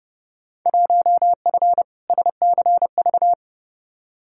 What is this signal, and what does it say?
Morse code, heard as a keyed tone, about 32 wpm, 700 Hz. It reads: 1F SCV
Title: Field Day Exchanges Encore - 30wpm